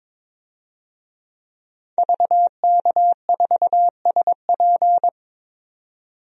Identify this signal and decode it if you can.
Morse code, heard as a keyed tone, about 22 wpm, 700 Hz. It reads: VK4SP
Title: DXCC-100 Call Signs - 5 Characters - 22wpm